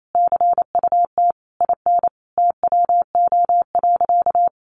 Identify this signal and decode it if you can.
Morse code, heard as a keyed tone, about 28 wpm, 700 Hz. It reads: CUT IN TWO.
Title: Sentences from Top 1300 Words 28wpm